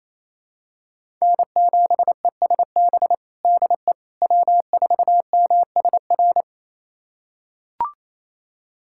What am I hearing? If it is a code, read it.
Morse code, heard as a keyed tone, about 28 wpm, 700 Hz. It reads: N7ESB DE W4MSR
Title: USA Callsign DE Callsign - 5 Characters - 28wpm